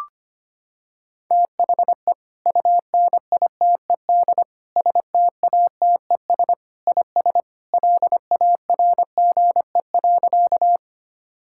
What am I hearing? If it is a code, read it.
Morse code, heard as a keyed tone, about 25 wpm, 700 Hz. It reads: THE UNITED STATES IS LARGE.
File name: practice / 25wpm / Sentences with 3-Word Phrases - 25wpm